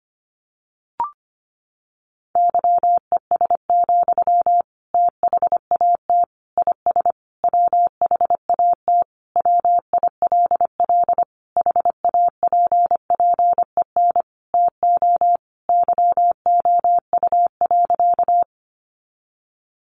Morse code, heard as a keyed tone, about 25 wpm, 700 Hz. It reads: YES, THAT IS WHAT WILL HAPPEN TO YOU.
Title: Sentences from Top 500 Words 25wpm